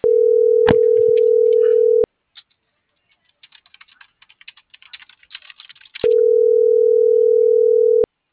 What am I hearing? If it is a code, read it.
Ringback tone (North America), 440 + 480 Hz.